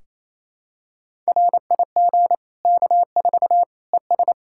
Morse code, heard as a keyed tone, about 28 wpm, 700 Hz. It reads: RIG K4 ES
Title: Practice QSO - Exchange 2 - Info - 28wpm